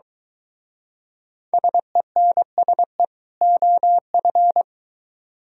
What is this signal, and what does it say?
Morse code, heard as a keyed tone, about 23 wpm, 700 Hz. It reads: SENSE OF